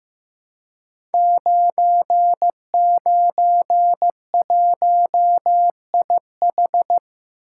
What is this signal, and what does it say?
Morse code, heard as a keyed tone, about 15 wpm, 700 Hz. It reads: 991IH
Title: Code Groups - 5 Letter:Numbers - 15wpm